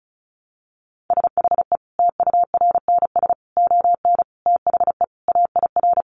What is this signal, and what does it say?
Morse code, heard as a keyed tone, about 34 wpm, 700 Hz. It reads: SHE TURNS ON THE AIR
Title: Easy Phrases 35wpm